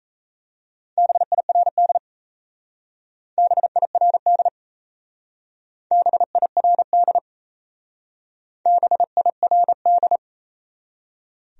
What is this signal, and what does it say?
Morse code, heard as a keyed tone, about 37 wpm, 700 Hz. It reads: BIRD BIRD BIRD BIRD